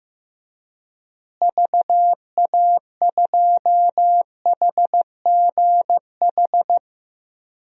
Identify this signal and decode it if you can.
Morse code, heard as a keyed tone, about 15 wpm, 700 Hz. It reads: VA2HGH
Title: DXCC-100 Call Signs - 6 Characters - Repeated 2x - 15wpm